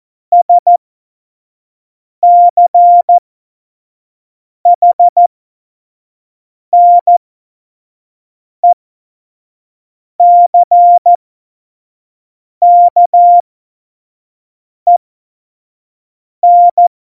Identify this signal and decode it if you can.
Morse code, heard as a keyed tone, about 14 wpm, spaced as for 5 wpm, 700 Hz. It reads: SCHNECKEN
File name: Morsekurs / Witze / Witz 04. Dezember 2018, 25 F70